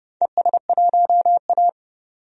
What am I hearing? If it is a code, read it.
Morse code, heard as a keyed tone, about 30 wpm, 700 Hz. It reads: ES1A
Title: DXCC-100 Call Signs - 4 Characters - 30wpm